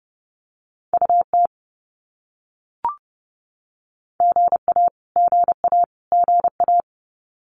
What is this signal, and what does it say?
Morse code, heard as a keyed tone, about 30 wpm, 700 Hz. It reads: UT GA GA GA
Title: US States Repeated 3x 30wpm